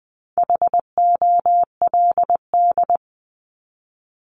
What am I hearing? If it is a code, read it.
Morse code, heard as a keyed tone, about 20 wpm, 700 Hz. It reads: HOLD
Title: ING Suffix - Top 100 Words with Base Words 20wpm